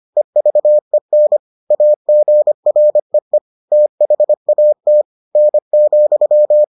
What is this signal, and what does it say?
Morse code, heard as a keyed tone, about 25 wpm, 600 Hz. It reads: EVEN AGREE THAT N,